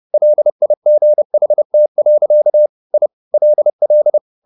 Morse code, heard as a keyed tone, about 30 wpm, 600 Hz. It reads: LIGHT. I LL
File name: the old man and the sea / the old man and the sea 1 30wpm